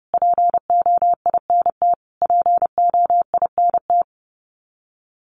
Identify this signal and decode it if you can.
Morse code, heard as a keyed tone, about 30 wpm, 700 Hz. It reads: POINT POINT
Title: Top 500 Words Repeated 30wpm